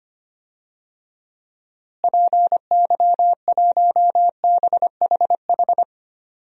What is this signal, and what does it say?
Morse code, heard as a keyed tone, about 25 wpm, 700 Hz. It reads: PY1BHH
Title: DXCC-100 Call Signs - 6 Characters - Repeated 2x - 25wpm